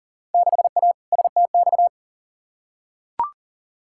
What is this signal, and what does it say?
Morse code, heard as a keyed tone, about 40 wpm, 700 Hz. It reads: 6A STX
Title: Field Day Exchanges - 40wpm